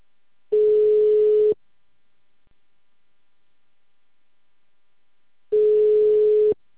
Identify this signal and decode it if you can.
Ringback tone (Europe, 425 Hz).
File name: out- -2044- - - .39749